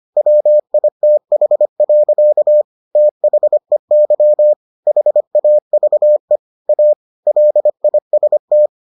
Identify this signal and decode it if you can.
Morse code, heard as a keyed tone, about 25 wpm, 600 Hz. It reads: WITH. THEY HAVE A LIST